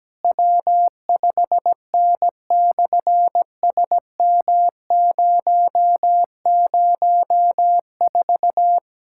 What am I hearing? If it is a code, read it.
Morse code, heard as a keyed tone, about 17 wpm, 700 Hz. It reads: W5N/SM004
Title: SOTA QSO Practice 17wpm